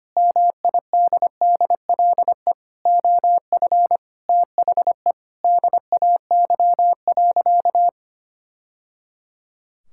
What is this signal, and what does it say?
Morse code, heard as a keyed tone, about 25 wpm, 700 Hz. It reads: MIDDLE OF THE DAY.
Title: Sentences from Top 1000 Words - Review Part 6 - 25wpm